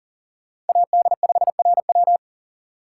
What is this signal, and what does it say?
Morse code, heard as a keyed tone, about 40 wpm, 700 Hz. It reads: AD5RW
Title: USA Call Signs - 5 Characters - 3x Character Spacing - 40wpm